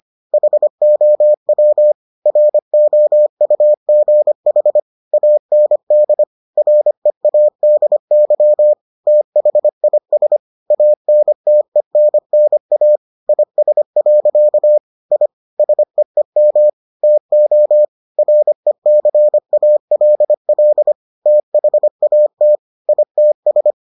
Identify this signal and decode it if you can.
Morse code, heard as a keyed tone, about 25 wpm, 600 Hz. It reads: HOW ROUGH AND READY THIS ANTENNA IS. I SEEM TO RECALL THAT ITS